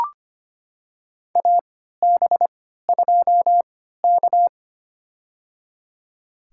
Morse code, heard as a keyed tone, about 25 wpm, spaced as for 14 wpm, 700 Hz. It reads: AB2K
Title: USA Call Signs - 4 Characters - 3x Character Spacing - 25wpm